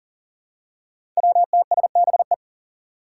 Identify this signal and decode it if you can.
Morse code, heard as a keyed tone, about 40 wpm, 700 Hz. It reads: WTSBE
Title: Five Challenging Letter-Number Combinations 40wpm